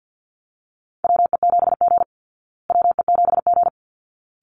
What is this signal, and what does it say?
Morse code, heard as a keyed tone, about 50 wpm, 700 Hz. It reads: WE7G WE7G
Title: Call Signs Repeated 3x - 50wpm